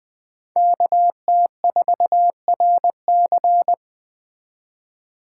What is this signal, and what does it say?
Morse code, heard as a keyed tone, about 21 wpm, 700 Hz. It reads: KT4RC